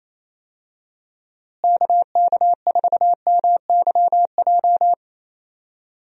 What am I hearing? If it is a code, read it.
Morse code, heard as a keyed tone, about 27 wpm, 700 Hz. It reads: KK4MYJ
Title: USA Call Signs - 6 Characters - Repeated 2x - 28wpm